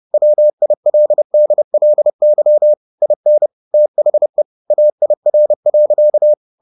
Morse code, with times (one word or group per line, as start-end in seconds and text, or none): 0.1-2.7 WILDLY
3.0-3.5 IN
3.7-4.4 THE
4.7-6.3 AIR.